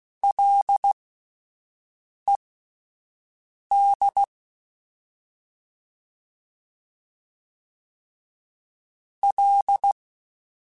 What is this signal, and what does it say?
Morse code, heard as a keyed tone, about 16 wpm, spaced as for 5 wpm, 800 Hz. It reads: LED L
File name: Ls / Single L m3s / Single L Prefixe 3x2 Twice @5